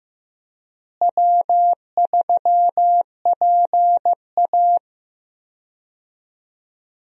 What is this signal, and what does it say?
Morse code, heard as a keyed tone, about 15 wpm, 700 Hz. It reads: W3PA
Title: USA Call Signs - 4 Characters - Repeated 3x - 15wpm